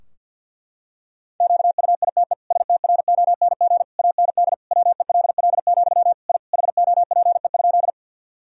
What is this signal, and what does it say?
Morse code, heard as a keyed tone, about 48 wpm, 700 Hz. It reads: QUITE STRONG AND WELL, I HOPE?
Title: Sentences from Top 1000 Words - Review Part 3 - 50wpm